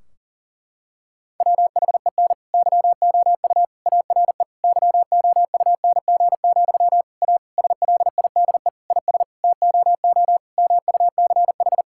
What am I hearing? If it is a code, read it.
Morse code, heard as a keyed tone, about 40 wpm, 700 Hz. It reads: WHEN YOU ARE YOUNG, A SLIDE IS TOO MUCH
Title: Sentences from Top 1600 Words 40wpm